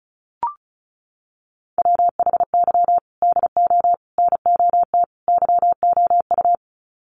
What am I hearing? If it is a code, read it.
Morse code, heard as a keyed tone, about 34 wpm, 700 Hz. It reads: WHY DO NOT YOU
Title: Sets of 4 Words 35wpm